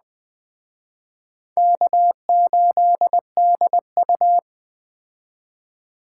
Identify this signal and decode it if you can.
Morse code, heard as a keyed tone, about 20 wpm, 700 Hz. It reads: K8DU
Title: Speed Racing - USA 1x2 Call Signs 17wpm